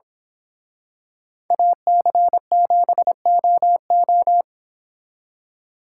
Morse code, heard as a keyed tone, about 26 wpm, 700 Hz. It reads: AC7OO
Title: Speed Racing - USA 2x2 Call Signs 22wpm